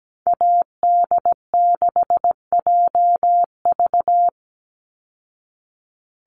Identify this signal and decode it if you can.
Morse code, heard as a keyed tone, about 17 wpm, 700 Hz. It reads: AD6JV
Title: Call Signs Repeated 3x - 17wpm